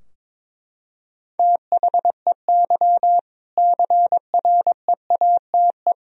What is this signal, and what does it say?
Morse code, heard as a keyed tone, about 21 wpm, 700 Hz. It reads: THEY CREATE
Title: Sentences from Top 1000 Words - Review Part 3 - 22wpm